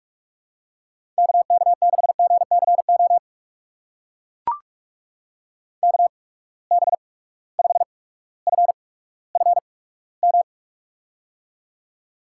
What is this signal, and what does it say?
Morse code, heard as a keyed tone, about 44 wpm, spaced as for 19 wpm, 700 Hz. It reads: KK6GCO KB5FFM
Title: USA Call Signs - 6 Characters - 8x Character Spacing - 45wpm